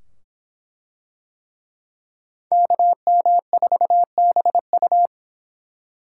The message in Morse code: KM4BU

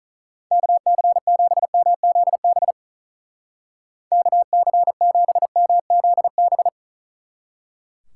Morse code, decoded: KC7MZB KC7MZB

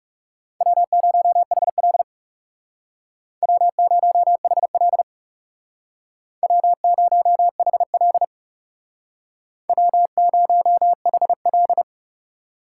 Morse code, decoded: W0HL W0HL W0HL W0HL

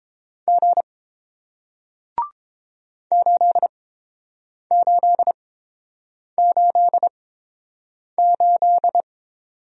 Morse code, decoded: G 8888

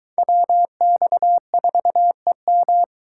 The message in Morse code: WX4EM

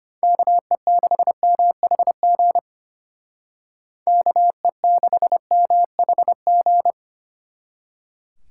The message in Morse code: KE6MHG KE6MHG

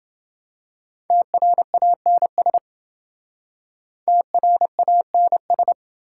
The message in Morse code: TRANS TRANS

